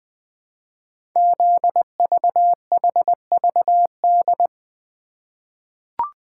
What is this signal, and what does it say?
Morse code, heard as a keyed tone, about 20 wpm, 700 Hz. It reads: ZVHVD
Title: Code Groups - 5 Letters - 20wpm